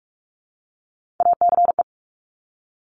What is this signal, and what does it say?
Morse code, heard as a keyed tone, about 46 wpm, 700 Hz. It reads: ACE
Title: Top Three Letter Combinations in Words 45wpm